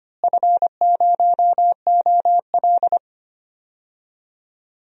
Morse code, heard as a keyed tone, about 25 wpm, 700 Hz. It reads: F0OL
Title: DXCC-100 Call Signs - 4 Characters - Repeated 2x - 25wpm